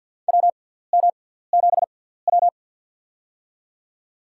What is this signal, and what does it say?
Morse code, heard as a keyed tone, about 49 wpm, spaced as for 17 wpm, 700 Hz. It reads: WM7W
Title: USA Call Signs - 6x Character Spacing - 50wpm